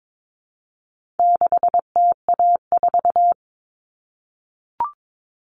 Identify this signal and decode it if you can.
Morse code, heard as a keyed tone, about 22 wpm, 700 Hz. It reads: 6TA4
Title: All Cut Numbers - 4-Digit 22wpm